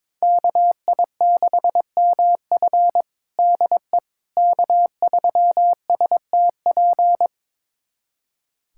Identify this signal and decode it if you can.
Morse code, heard as a keyed tone, about 22 wpm, 700 Hz. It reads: KI6MF DE K3STP
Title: USA Callsign DE Callsign - 5 Characters - 22wpm